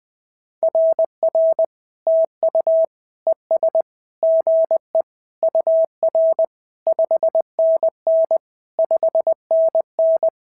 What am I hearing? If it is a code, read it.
Morse code, heard as a keyed tone, about 19 wpm, 650 Hz. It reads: RR TU ES GE UR 5NN 5NN
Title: SOTA QSO Practice 20wpm